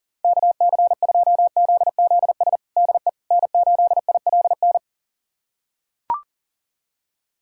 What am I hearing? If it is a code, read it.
Morse code, heard as a keyed tone, about 40 wpm, 700 Hz. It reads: KC2ZZS DE N8ILN
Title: USA Callsign DE Callsign 40wpm